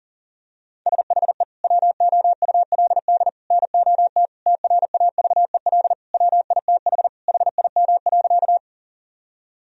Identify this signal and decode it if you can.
Morse code, heard as a keyed tone, about 40 wpm, 700 Hz. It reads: SHE WOULD NOT TRAVEL WITH HIM.
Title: Sentences from Top 1000 Words - Review Part 2 - 40wpm